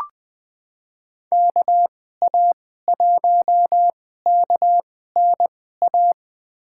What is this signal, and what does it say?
Morse code, heard as a keyed tone, about 20 wpm, spaced as for 14 wpm, 700 Hz. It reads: KA1KNA
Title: USA Call Signs - 6 Characters - 2x Character Spacing - 20wpm